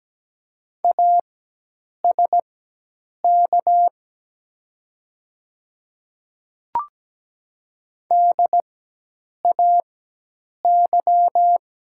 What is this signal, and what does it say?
Morse code, heard as a keyed tone, about 17 wpm, spaced as for 8 wpm, 700 Hz. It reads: ASK DAY
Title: 4x Character Spacing - Top 200 Words - 17wpm